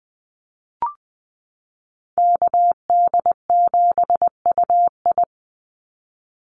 Morse code, heard as a keyed tone, about 20 wpm, 700 Hz. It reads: KD7UI
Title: Call Signs - 20wpm